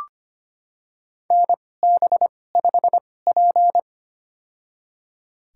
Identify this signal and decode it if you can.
Morse code, heard as a keyed tone, about 25 wpm, spaced as for 18 wpm, 700 Hz. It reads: NB5P
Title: USA Call Signs - 4 Characters - 2x Character Spacing - 25wpm